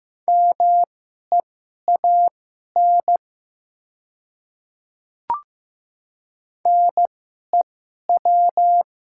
Morse code, heard as a keyed tone, about 15 wpm, spaced as for 11 wpm, 700 Hz. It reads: MEAN NEW
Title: 2x Character Spacing - Top 200 Words - 15wpm